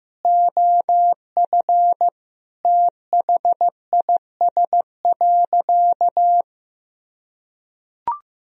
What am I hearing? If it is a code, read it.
Morse code, heard as a keyed tone, about 15 wpm, 700 Hz. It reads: OF THIS.
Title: Sentences from Top 1000 Words - Review Part 2 - 15wpm